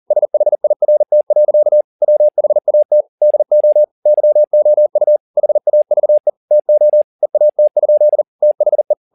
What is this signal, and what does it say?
Morse code, heard as a keyed tone, about 40 wpm, 600 Hz. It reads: SHIRT. WHAT DO YOU HAVE TO EAT? THE